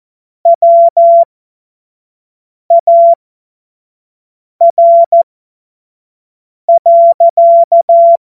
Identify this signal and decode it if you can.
Morse code, heard as a keyed tone, about 14 wpm, spaced as for 5 wpm, 680 Hz. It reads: WAR.
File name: Morsekurs / Witze / Witz 21. März 2023 25 F70